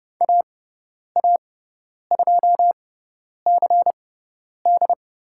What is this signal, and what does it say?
Morse code, heard as a keyed tone, about 30 wpm, spaced as for 10 wpm, 700 Hz. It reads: AA2CD
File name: Lesson 18g - Letter m - Call Signs 30wpm